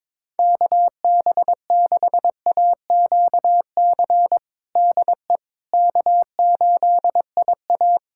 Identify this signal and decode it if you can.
Morse code, heard as a keyed tone, about 21 wpm, 700 Hz. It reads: KB6AQC DE K8IA
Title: K1USN SST QSO Practice 22wpm